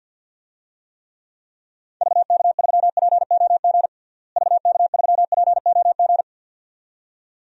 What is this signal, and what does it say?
Morse code, heard as a keyed tone, about 50 wpm, 700 Hz. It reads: VK3POG VK3POG
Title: DXCC-100 Call Signs - 6 Characters - Repeated 2x - 50wpm